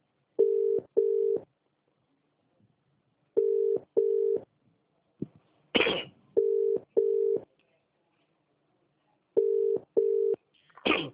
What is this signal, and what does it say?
Ringback tone (UK), 400 + 450 Hz.